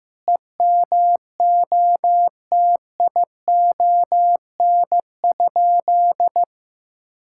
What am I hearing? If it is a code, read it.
Morse code, heard as a keyed tone, about 16 wpm, 700 Hz. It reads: EMOTION?
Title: Sentences from Top 1600 Words 15wpm